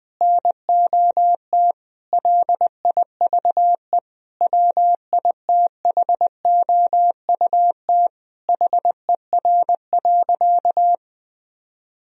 Morse code, heard as a keyed tone, about 20 wpm, 700 Hz. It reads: NOT LIVE WITHOUT HER.
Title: Sentences from Top 1000 Words - Review Part 3 - 20wpm